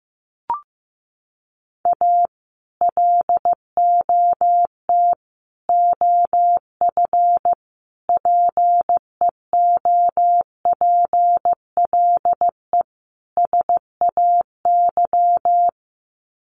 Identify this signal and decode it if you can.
Morse code, heard as a keyed tone, about 15 wpm, 700 Hz. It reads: A LOT OF PEOPLE SAY
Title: Sets of 5 Words Encore2 15wpm